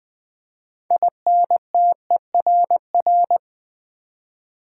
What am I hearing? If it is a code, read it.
Morse code, heard as a keyed tone, about 20 wpm, 700 Hz. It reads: INTERR
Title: Build A Word - 3 Letters At A Time - 9-Letter Words - 20wpm